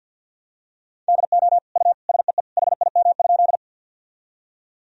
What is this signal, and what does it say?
Morse code, heard as a keyed tone, about 46 wpm, 700 Hz. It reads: DO U SEE HIM?
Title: Sentences from Top 100 Words and Abbreviations 50wpm